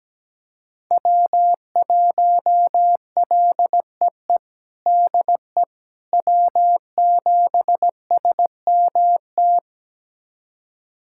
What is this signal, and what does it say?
Morse code, heard as a keyed tone, about 17 wpm, 700 Hz. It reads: W1LEE DE W7SMT
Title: USA Callsign DE Callsign - 5 Characters - 17wpm